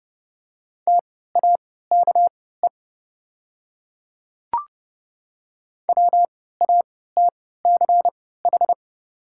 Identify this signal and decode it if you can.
Morse code, heard as a keyed tone, about 30 wpm, spaced as for 17 wpm, 690 Hz. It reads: TAKE WATCH
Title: 3x Character Spacing - Top 500 Words - 30wpm